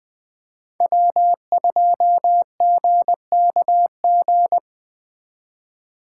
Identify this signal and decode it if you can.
Morse code, heard as a keyed tone, about 20 wpm, 700 Hz. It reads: W2GKG
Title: Speed Racing - USA 1x3 Call Signs 17wpm